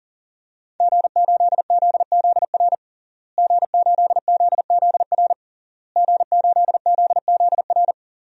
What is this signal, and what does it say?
Morse code, heard as a keyed tone, about 40 wpm, 700 Hz. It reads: G8ZZR G8ZZR G8ZZR
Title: DXCC-100 Call Signs - 5 Characters - Repeated 3x - 40wpm